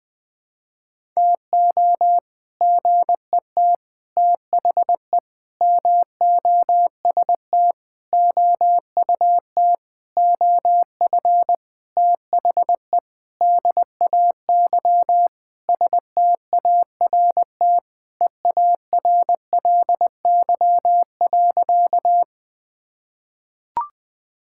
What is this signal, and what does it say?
Morse code, heard as a keyed tone, about 19 wpm, 700 Hz. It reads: TO GET THE MOST OUT OF THE DAY START EARLY.
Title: Sentences with 7-Word Phrases - Spoken Beginning - 20wpm